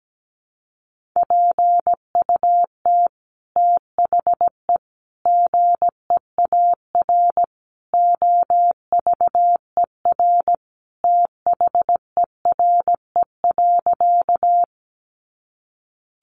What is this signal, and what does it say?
Morse code, heard as a keyed tone, about 17 wpm, 700 Hz. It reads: PUT THE GEAR OVER THERE.
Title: Sentences from Top 1400 Words 17wpm